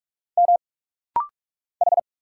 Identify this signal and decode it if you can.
Morse code, heard as a keyed tone, about 46 wpm, 700 Hz. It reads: M H
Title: Single Letter-Number - Rapid-Fire 45wpm